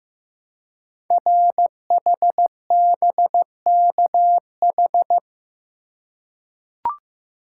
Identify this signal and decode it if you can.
Morse code, heard as a keyed tone, about 15 wpm, 700 Hz. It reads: RHBKH
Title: Code Groups - 5 Letters - 15wpm